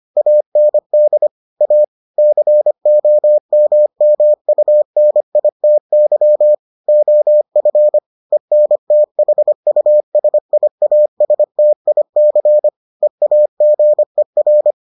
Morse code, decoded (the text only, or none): AND A COMMUNITY OF ENTHUSIASTIC EAGER